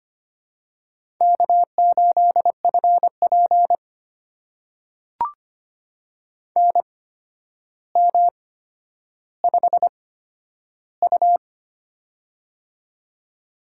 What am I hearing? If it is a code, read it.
Morse code, heard as a keyed tone, about 25 wpm, spaced as for 11 wpm, 700 Hz. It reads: K8FP NM5U
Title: USA Call Signs - 4 Characters - 8x Character Spacing - 25wpm